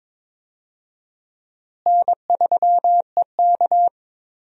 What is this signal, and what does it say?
Morse code, heard as a keyed tone, about 21 wpm, 700 Hz. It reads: N3EK
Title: USA Call Signs - 4 Characters - Repeated 2x - 22wpm